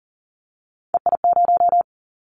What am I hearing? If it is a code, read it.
Morse code, heard as a keyed tone, about 39 wpm, 700 Hz. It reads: EI0
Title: Callsign Prefixes for Top 50 DXCC - 40wpm